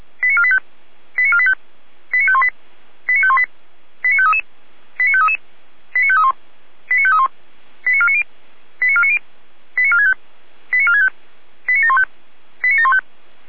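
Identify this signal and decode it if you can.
Five-tone selective call (ZVEI), heard as quick runs of five tones, tones of about 70 ms. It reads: 89485 89485 89518 89518 89533 89533 89531 89531 89490 89490 89465 89465 89715 89715
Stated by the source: Received on an SDR.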